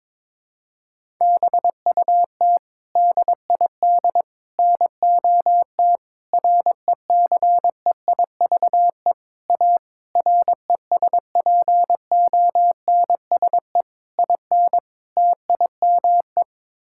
Morse code, heard as a keyed tone, about 21 wpm, 700 Hz. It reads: BUT DID NOT RECEIVE A RESPONSE IN TIME
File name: practice / 22wpm / Sets of 8 Words - 22wpm